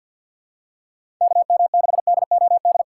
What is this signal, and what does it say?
Morse code, heard as a keyed tone, about 52 wpm, 700 Hz. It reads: KM6DOD
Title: USA Call Signs - 6 Characters - 4x Character Spacing - 50wpm